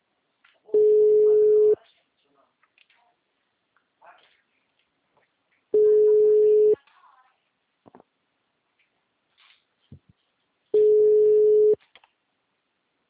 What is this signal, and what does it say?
Ringback tone (Europe, 425 Hz), with speech.